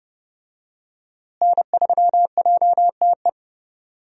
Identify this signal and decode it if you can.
Morse code, heard as a keyed tone, about 31 wpm, 700 Hz. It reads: N3JTE
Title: USA Call Signs - 5 Characters - Repeated 3x - 30wpm